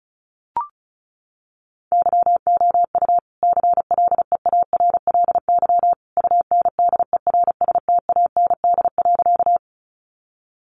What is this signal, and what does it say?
Morse code, heard as a keyed tone, about 34 wpm, 700 Hz. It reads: YOU CLEARLY UNDERSTAND.
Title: Sentences from Top 1300 Words 35wpm